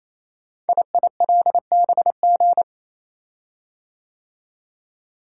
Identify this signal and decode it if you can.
Morse code, heard as a keyed tone, about 29 wpm, 700 Hz. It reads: IILBG
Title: Code Groups - 5 Letters - 28wpm